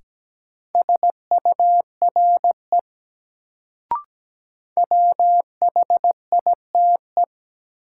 Morse code, heard as a keyed tone, about 17 wpm, 700 Hz. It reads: SURE WHITE
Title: Sentences from Top 300 Words - New Word Drill - 17wpm